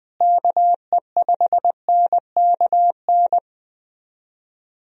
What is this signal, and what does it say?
Morse code, heard as a keyed tone, about 20 wpm, 700 Hz. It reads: KE5NKN
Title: USA Call Signs - 6 Characters - Repeated 3x - 20wpm